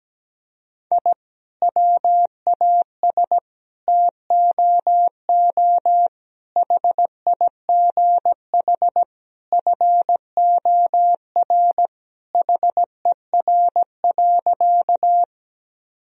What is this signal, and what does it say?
Morse code, heard as a keyed tone, about 17 wpm, 700 Hz. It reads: I WAS TOO HIGH FOR HER.
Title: Sentences from Top 1000 Words - Review Part 1 - 17wpm